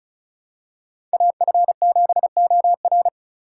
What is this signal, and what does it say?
Morse code, heard as a keyed tone, about 36 wpm, 700 Hz. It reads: AF7OR
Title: USA Call Signs - 5 Characters - 4x Character Spacing - 35wpm